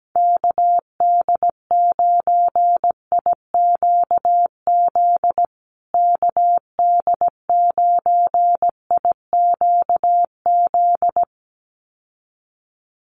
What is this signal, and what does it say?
Morse code, heard as a keyed tone, about 17 wpm, 700 Hz. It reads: KD9IQZ KD9IQZ
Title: USA Call Signs Repeated 2x 17wpm